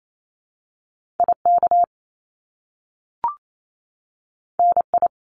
Morse code, heard as a keyed tone, about 28 wpm, 700 Hz. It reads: IK NI
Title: Top Two Letter Combinations in Words 28wpm